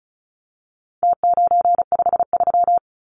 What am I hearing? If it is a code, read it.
Morse code, heard as a keyed tone, about 34 wpm, 700 Hz. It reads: T953
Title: Numbers - Contest Style - 4-Digit 35wpm